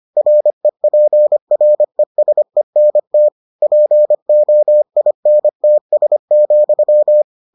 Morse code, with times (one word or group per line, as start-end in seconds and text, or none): 0.2-3.3 REPRESENT
3.6-7.2 POINTS,